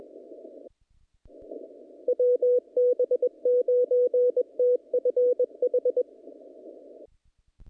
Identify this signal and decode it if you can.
Morse code, heard as a keyed tone, about 22 wpm, 500 Hz. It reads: WB9TFH
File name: contests / naqp-cw-k5tr-op-k5pi- -2000